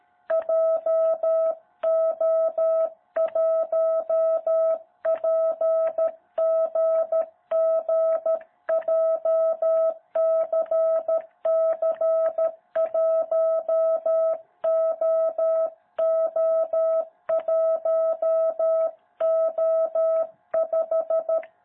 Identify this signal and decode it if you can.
Morse code, heard as a keyed tone, about 13 wpm, 660 Hz. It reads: JO1PGG JCC1OO1O5
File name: jh8cbh / 2021cw